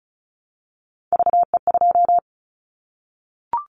Morse code, heard as a keyed tone, about 34 wpm, 700 Hz. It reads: VE2
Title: Callsign Prefixes for Top 100 DXCC - 35wpm